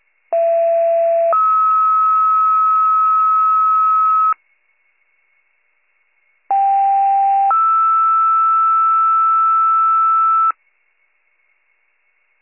Two-tone page, heard as a one-second tone, then a three-second tone, about 670 then 1290 Hz, about 790 then 1320 Hz. It is heard from 0.3 to 10.5 s.